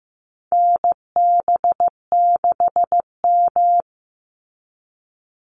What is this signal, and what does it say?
Morse code, heard as a keyed tone, about 15 wpm, 700 Hz. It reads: NB6M
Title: Call Signs Repeated 2x - 15wpm